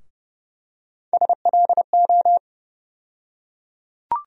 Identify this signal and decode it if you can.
Morse code, heard as a keyed tone, about 30 wpm, 700 Hz. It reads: SLO
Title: USA Call Sign Postfixes - 3 Letters 30wpm